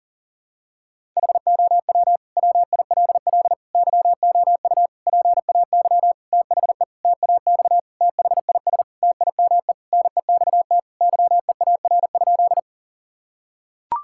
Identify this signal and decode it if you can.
Morse code, heard as a keyed tone, about 40 wpm, 700 Hz. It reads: HOW WILL YOU PAY THE TAX THIS TIME NEXT YEAR?
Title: Sentences from Top 1000 Words - Review Part 5 - 40wpm